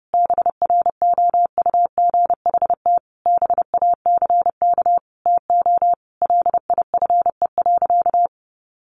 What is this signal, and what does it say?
Morse code, heard as a keyed tone, about 30 wpm, 700 Hz. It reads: BROUGHT BACK TO LIFE.
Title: Sentences from Top 600 Words 30wpm